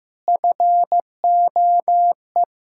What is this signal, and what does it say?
Morse code, heard as a keyed tone, about 16 wpm, 700 Hz. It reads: FOE
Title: Pangram Challenge v2 - 15wpm